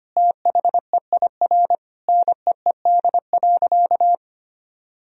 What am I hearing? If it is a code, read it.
Morse code, heard as a keyed tone, about 25 wpm, 700 Hz. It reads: THEIR NEED.
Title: Sentences from Top 1000 Words - Review Part 4 - 25wpm